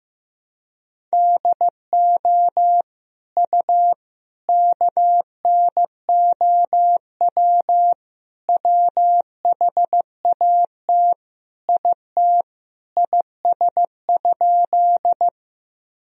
Morse code, heard as a keyed tone, about 15 wpm, 700 Hz. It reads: DO U KNOW WHAT IT IS?
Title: Sentences from Top 100 Words and Abbreviations 15wpm